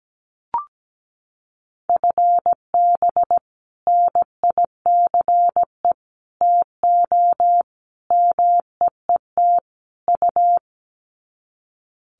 Morse code, heard as a keyed tone, about 17 wpm, 700 Hz. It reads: FB NICE TO MEET U
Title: Common QSO Phrases 17wpm